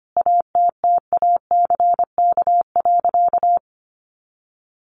Morse code, heard as a keyed tone, about 25 wpm, 700 Hz. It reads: ATTACK.
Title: Sentences from Top 1500 Words 25wpm